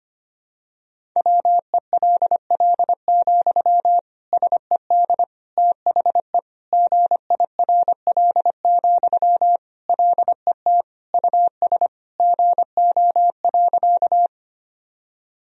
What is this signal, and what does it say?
Morse code, heard as a keyed tone, about 25 wpm, 700 Hz. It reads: WELL, SED THE GIRL, LET US GO.
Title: Sentences from Top 400 Words and Abbreviations 25wpm